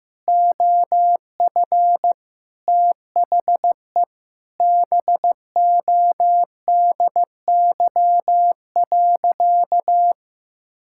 Morse code, heard as a keyed tone, about 16 wpm, 700 Hz. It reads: OF THE BODY.
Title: Sentences from Top 1000 Words - Review Part 6 - 15wpm